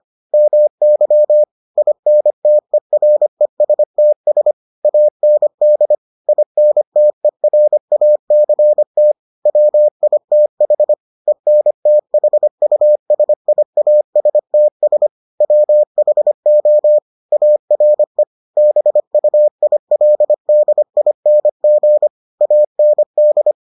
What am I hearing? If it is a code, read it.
Morse code, heard as a keyed tone, about 25 wpm, 600 Hz. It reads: MY INTERESTS AND INTERACT WITH ENTHUSIASTS WHO ARE BUILDING AND